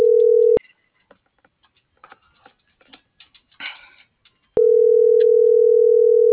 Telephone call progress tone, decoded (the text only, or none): ringback tone